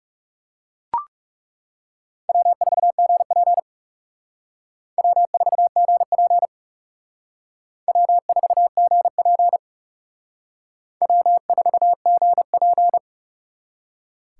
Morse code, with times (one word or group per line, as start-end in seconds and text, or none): 2.3-3.6 W4GP
5.0-6.5 W4GP
7.9-9.6 W4GP
11.0-13.0 W4GP